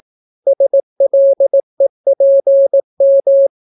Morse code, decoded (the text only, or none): SLEPM